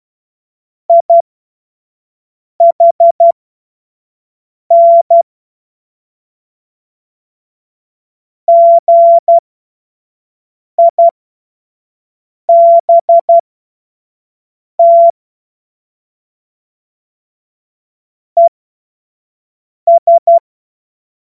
Morse code, heard as a keyed tone, about 12 wpm, spaced as for 5 wpm, 680 Hz. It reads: IHN GIBT ES
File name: Morsekurs / Witze / Witz 07. Juni 2022 25 F70